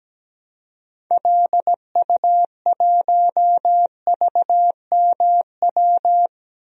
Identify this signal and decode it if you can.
Morse code, heard as a keyed tone, about 17 wpm, 700 Hz. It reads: LU1VMW
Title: DXCC-100 Call Signs - 6 Characters - Repeated 3x - 17wpm